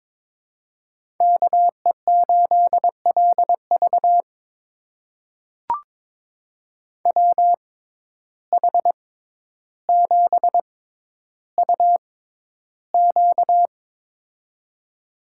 Morse code, heard as a keyed tone, about 22 wpm, spaced as for 11 wpm, 700 Hz. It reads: KE8LV WH7UQ
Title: USA Call Signs - 5 Characters - 6x Character Spacing - 22wpm